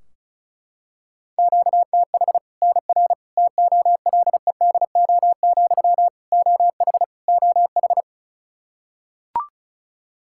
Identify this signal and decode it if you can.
Morse code, heard as a keyed tone, about 36 wpm, 700 Hz. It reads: QTH NR TOLEDO, OH OH
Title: Practice QSO - Exchange 1 - Beginning - 35wpm